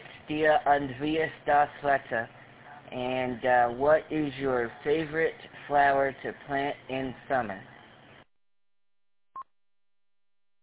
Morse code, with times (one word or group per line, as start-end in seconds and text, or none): none